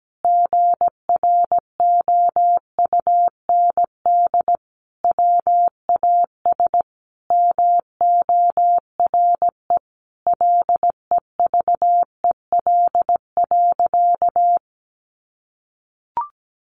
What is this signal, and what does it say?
Morse code, heard as a keyed tone, about 17 wpm, 700 Hz. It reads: GROUND WAS MORE LEVEL.
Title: Sentences from Top 700 Words 17wpm